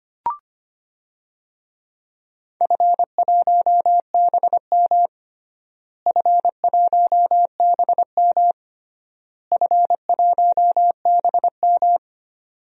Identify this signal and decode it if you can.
Morse code, heard as a keyed tone, about 25 wpm, 700 Hz. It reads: F1BM F1BM F1BM
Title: DXCC-100 Call Signs - 4 Characters - Repeated 3x - 25wpm